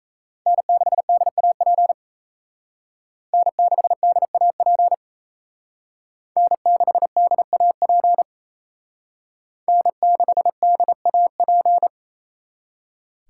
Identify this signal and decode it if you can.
Morse code, heard as a keyed tone, about 35 wpm, 700 Hz. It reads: N6DAP N6DAP N6DAP N6DAP